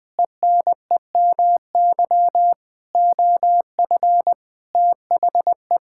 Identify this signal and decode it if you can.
Morse code, heard as a keyed tone, about 20 wpm, 700 Hz. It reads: ENEMY OF THE
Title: Sentences with 9-Word Phrases - 20wpm